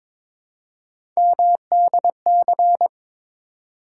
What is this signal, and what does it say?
Morse code, heard as a keyed tone, about 22 wpm, 700 Hz. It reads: MDC
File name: ARRL Sections v2 - 22wpm